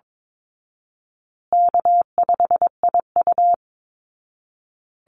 Morse code, heard as a keyed tone, about 22 wpm, 700 Hz. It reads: K5IU
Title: Easy Call Signs Repeated 3x - 22wpm